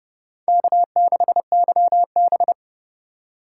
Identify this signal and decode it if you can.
Morse code, heard as a keyed tone, about 38 wpm, 700 Hz. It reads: K6YB